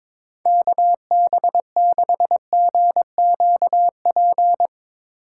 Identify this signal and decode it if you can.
Morse code, heard as a keyed tone, about 22 wpm, 700 Hz. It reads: KB6GQP
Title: USA Call Signs - 6 Characters - 8x Character Spacing - 22wpm